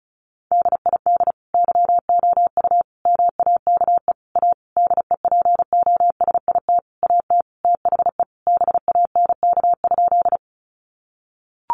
Morse code, heard as a keyed tone, about 34 wpm, 700 Hz. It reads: DID YOU MAKE A DEPOSIT AT THE BANK?
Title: Sentences from Top 1300 Words 35wpm